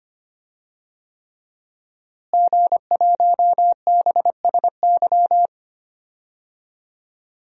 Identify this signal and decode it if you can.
Morse code, heard as a keyed tone, about 25 wpm, 700 Hz. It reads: G1BSY
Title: DXCC-100 Call Signs - 5 Characters - Repeated 3x - 25wpm